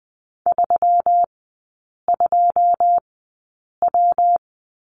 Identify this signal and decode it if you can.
Morse code, heard as a keyed tone, about 20 wpm, spaced as for 8 wpm, 700 Hz. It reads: 32W
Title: 138 Lesson 23e - Letter f - Three Characters 20wpm